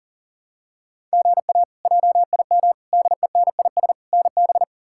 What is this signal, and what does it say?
Morse code, heard as a keyed tone, about 40 wpm, 700 Hz. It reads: GA JIM DENIS NB
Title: K1USN SST QSO Practice - 40wpm